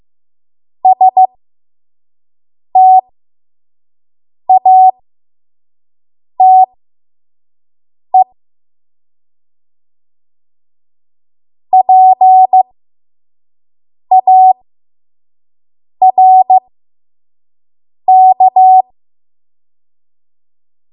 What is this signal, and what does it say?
Morse code, heard as a keyed tone, about 15 wpm, spaced as for 5 wpm, 750 Hz. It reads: STATE PARK
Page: 5 WPM Code Archive